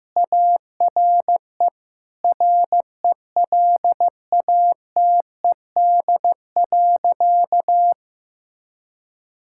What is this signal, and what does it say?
Morse code, heard as a keyed tone, about 15 wpm, 700 Hz. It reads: ARE RELATED.
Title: Sentences from Top 1000 Words - Review Part 5 - 15wpm